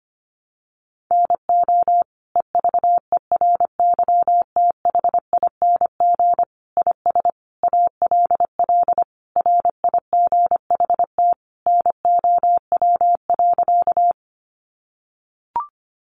Morse code, heard as a keyed tone, about 25 wpm, 700 Hz. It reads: NO EVERYTHING IS ALL RIGHT NOW.
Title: Sentences from Top 700 Words 25wpm